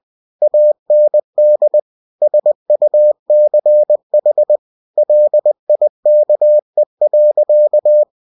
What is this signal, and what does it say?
Morse code, heard as a keyed tone, about 20 wpm, 600 Hz. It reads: AND SUCH LIKE.
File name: cw-books / winnie-the-pooh / winnie-20wpm-page-0013